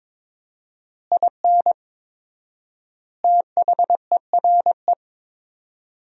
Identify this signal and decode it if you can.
Morse code, heard as a keyed tone, about 22 wpm, 700 Hz. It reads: IN THERE
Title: 4x Word-Spacing - Sentences from Top 500 Words - 22wpm